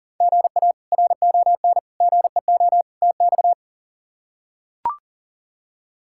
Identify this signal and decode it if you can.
Morse code, heard as a keyed tone, about 40 wpm, 700 Hz. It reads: GA RON GEO TX
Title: K1USN SST QSO Practice - 40wpm